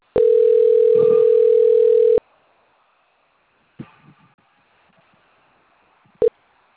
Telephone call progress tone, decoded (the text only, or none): ringback tone